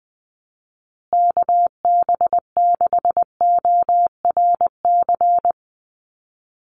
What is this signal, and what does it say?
Morse code, heard as a keyed tone, about 20 wpm, 700 Hz. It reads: KB6ORC
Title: USA Call Signs Repeated 2x 20wpm